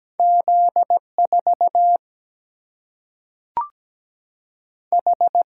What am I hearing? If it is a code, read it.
Morse code, heard as a keyed tone, about 17 wpm, 700 Hz. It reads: Z 4 H
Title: Single-Letter Single-Number 17wpm